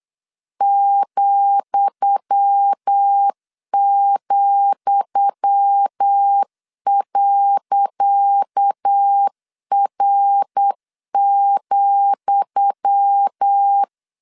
Morse code, heard as a keyed tone, about 8 wpm, 800 Hz. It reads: ,,.R,
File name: LEZIONI / 10 SIGNS / 10D COMA 8wpm 3749